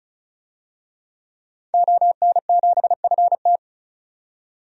Morse code, heard as a keyed tone, about 36 wpm, 700 Hz. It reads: ON7FT
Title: DXCC-100 Call Signs - 5 Characters - 35wpm